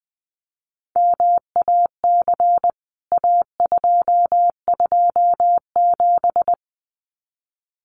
Morse code, heard as a keyed tone, about 20 wpm, 700 Hz. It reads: MAC A227
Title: CWT Exchange 20wpm